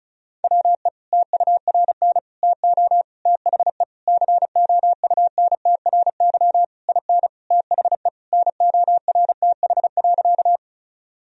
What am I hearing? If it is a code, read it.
Morse code, heard as a keyed tone, about 36 wpm, 700 Hz. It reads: WE TURN TO THE COUNTRY IN THE NORTH.
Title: Sentences from Top 1000 Words - Review Part 1 - 35wpm